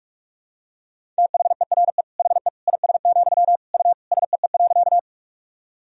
Morse code, heard as a keyed tone, about 46 wpm, 700 Hz. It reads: THERE HE IS, U SEE.
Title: Sentences from Top 100 Words and Abbreviations 45wpm